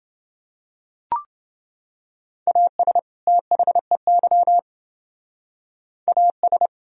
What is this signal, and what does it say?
Morse code, heard as a keyed tone, about 30 wpm, 700 Hz. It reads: AS THEY AS